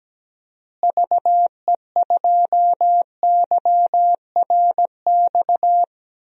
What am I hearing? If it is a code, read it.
Morse code, heard as a keyed tone, about 17 wpm, 700 Hz. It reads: VE2YRX
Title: DXCC-100 Call Signs - 6 Characters - Repeated 3x - 17wpm